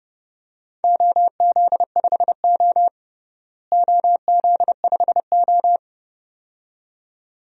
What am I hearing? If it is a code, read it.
Morse code, heard as a keyed tone, about 30 wpm, 700 Hz. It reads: OZ5O OZ5O
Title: DXCC-100 Call Signs - 4 Characters - Repeated 2x - 30wpm